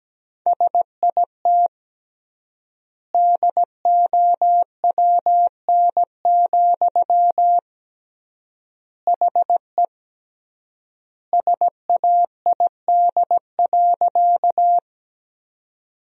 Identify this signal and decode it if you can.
Morse code, heard as a keyed tone, about 17 wpm, 700 Hz. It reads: SIT DOWN, HE SAID.
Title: 3x Word-Spacing - Sentences from Top 500 Words - 17wpm